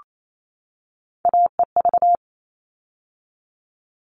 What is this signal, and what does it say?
Morse code, heard as a keyed tone, about 28 wpm, 700 Hz. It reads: AEV
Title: All Cut Numbers - 3-Digit 28wpm